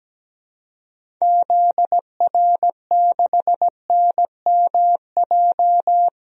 Morse code, heard as a keyed tone, about 17 wpm, 700 Hz. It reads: ZR6NMJ
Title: DXCC-100 Call Signs - 6 Characters - 17wpm